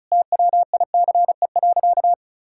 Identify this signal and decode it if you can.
Morse code, heard as a keyed tone, about 36 wpm, 700 Hz. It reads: TWICE.
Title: Sentences from Top 1600 Words 35wpm